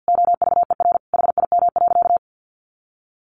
Morse code, heard as a keyed tone, about 50 wpm, 700 Hz. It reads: OVER HIM.
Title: Sentences from Top 300 Words 50wpm